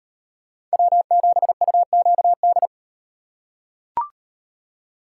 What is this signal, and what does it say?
Morse code, heard as a keyed tone, about 38 wpm, 700 Hz. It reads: W7UQD